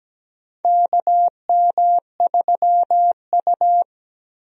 Morse code, heard as a keyed tone, about 17 wpm, 700 Hz. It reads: KM3U
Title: USA Call Signs - 4 Characters - Repeated 2x - 17wpm